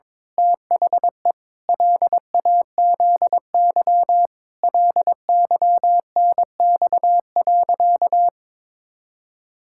Morse code, heard as a keyed tone, about 22 wpm, 700 Hz. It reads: THE LAZY LYNX.
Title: Pangram Challenge v2 - 22wpm